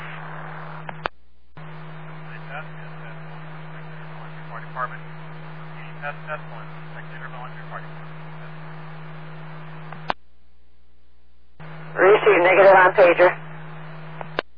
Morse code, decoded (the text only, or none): none